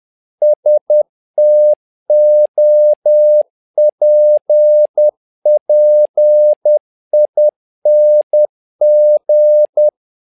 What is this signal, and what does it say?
Morse code, heard as a keyed tone, about 10 wpm, 600 Hz. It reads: STOPPING
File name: learning center / tower trouble / tower trouble 10wpm